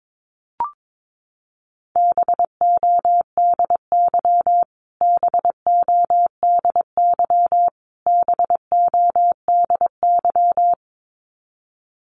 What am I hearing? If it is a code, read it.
Morse code, heard as a keyed tone, about 22 wpm, 700 Hz. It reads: BODY BODY BODY
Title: Top 500 Words Repeated 22wpm